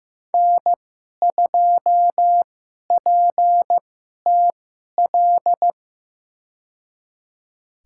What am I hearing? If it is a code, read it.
Morse code, heard as a keyed tone, about 15 wpm, spaced as for 11 wpm, 700 Hz. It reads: N2PTL
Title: USA Call Signs - 2x Character Spacing - 15wpm